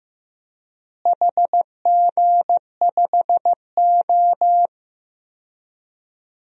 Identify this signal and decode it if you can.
Morse code, heard as a keyed tone, about 15 wpm, 700 Hz. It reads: HG5O
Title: DXCC-100 Call Signs - 4 Characters - Repeated 2x - 15wpm